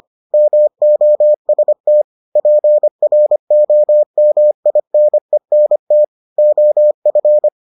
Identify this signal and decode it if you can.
Morse code, heard as a keyed tone, about 25 wpm, 600 Hz. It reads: MOST PROMINENT OF